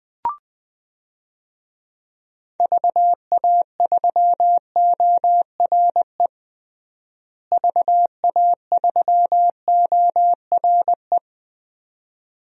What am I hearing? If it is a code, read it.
Morse code, heard as a keyed tone, about 20 wpm, 700 Hz. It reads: VA3ORE VA3ORE
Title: DXCC-100 Call Signs - 6 Characters - Repeated 2x - 20wpm